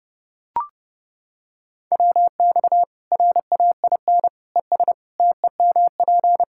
Morse code, heard as a keyed tone, about 32 wpm, 700 Hz. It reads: WX RAIN ES TEMP
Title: Practice QSO - Exchange 2 - Info - 30wpm